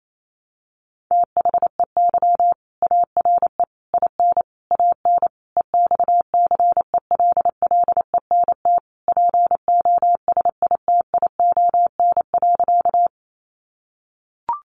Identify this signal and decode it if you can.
Morse code, heard as a keyed tone, about 28 wpm, 700 Hz. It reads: THEY ARE IN AN EXCELLENT POSITION.
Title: Sentences from Top 1000 Words 28wpm